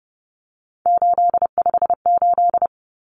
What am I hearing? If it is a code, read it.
Morse code, heard as a keyed tone, about 30 wpm, 700 Hz. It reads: 858
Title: Numbers - 3-Digit 30wpm